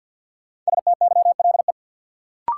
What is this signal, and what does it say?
Morse code, heard as a keyed tone, about 50 wpm, 700 Hz. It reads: STYLE
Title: Sentences from Top 800 Words - New Word Drill - 50wpm